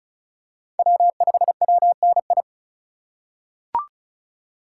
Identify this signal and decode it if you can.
Morse code, heard as a keyed tone, about 34 wpm, 700 Hz. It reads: W5WNI
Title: USA Call Signs - 5 Characters - 3x Character Spacing - 35wpm